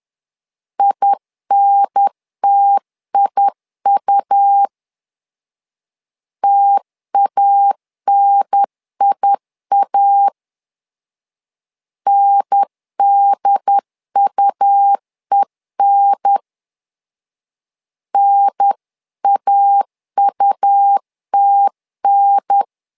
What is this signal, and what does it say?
Morse code, heard as a keyed tone, about 10 wpm, 800 Hz. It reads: INTIU TANIA NDUEN NAUTN
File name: LEZIONI / 2 UHDN / 2 UHDN 10wpm 1904